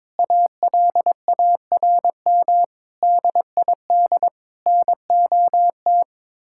Morse code, heard as a keyed tone, about 22 wpm, 700 Hz. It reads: ALARM DID NOT
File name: practice / 22wpm / Sentences with 4-Word Phrases - 22wpm